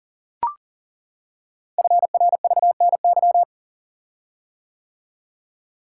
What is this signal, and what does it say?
Morse code, heard as a keyed tone, about 42 wpm, 690 Hz. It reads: FRVNY
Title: Code Groups - 5 Letters - 40wpm